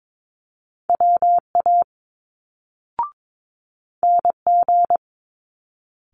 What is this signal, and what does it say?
Morse code, heard as a keyed tone, about 22 wpm, 700 Hz. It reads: WA NG
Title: Multi-letter Phonemes Quick Fire 22wpm